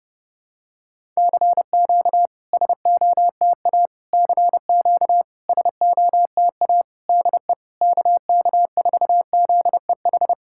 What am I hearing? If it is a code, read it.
Morse code, heard as a keyed tone, about 30 wpm, 690 Hz. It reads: CQ SOTA CQ SOTA DE KK4ZEH
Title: SOTA QSO Practice 30wpm